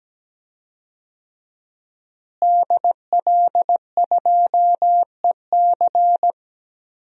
Morse code, heard as a keyed tone, about 17 wpm, 700 Hz. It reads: DL2EC
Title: DXCC-100 Call Signs - 5 Characters - Repeated 3x - 17wpm